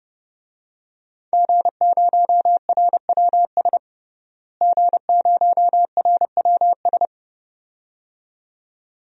Morse code, heard as a keyed tone, about 30 wpm, 700 Hz. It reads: G0RWS G0RWS
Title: DXCC-100 Call Signs - 5 Characters - Repeated 2x - 30wpm